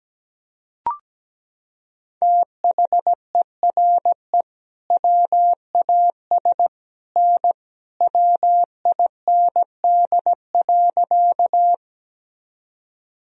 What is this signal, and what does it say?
Morse code, heard as a keyed tone, about 17 wpm, 700 Hz. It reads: THERE WAS N WIND.
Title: Sentences from Top 400 Words and Abbreviations 17wpm